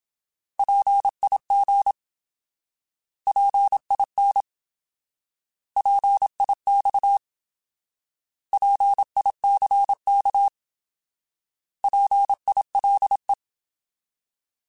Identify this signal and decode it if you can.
Morse code, heard as a keyed tone, about 27 wpm, 800 Hz. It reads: PIG PIN PIX PICK PILE
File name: Alphabetical / Alphabetical mp3s / Ps @25